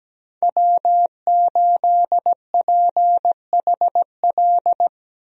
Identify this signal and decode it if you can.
Morse code, heard as a keyed tone, about 17 wpm, 700 Hz. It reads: W8PHL
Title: USA Call Signs - 5 Characters - 17wpm